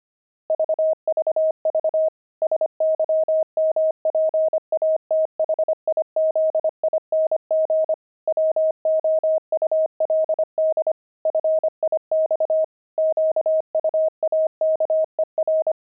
Morse code, heard as a keyed tone, about 25 wpm, 630 Hz. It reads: VVV SYMPATHIZING WOULD FIX QUAKER
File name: pangram-fs-25-cs-25